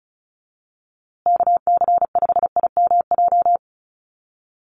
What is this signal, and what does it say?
Morse code, heard as a keyed tone, about 34 wpm, 700 Hz. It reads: KC5IMJ
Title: USA Call Signs Repeated 3x 35wpm